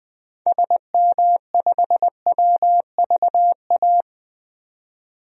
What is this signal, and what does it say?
Morse code, heard as a keyed tone, about 20 wpm, 700 Hz. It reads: SM5WVA
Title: DXCC-100 Call Signs - 6 Characters - 20wpm